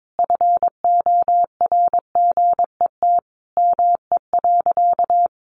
Morse code, heard as a keyed tone, about 22 wpm, 700 Hz. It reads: FORGET ME.
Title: Sentences from Top 1200 Words 22wpm